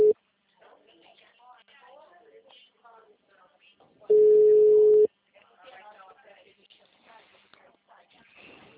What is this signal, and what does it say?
Ringback tone (Europe, 425 Hz), with speech.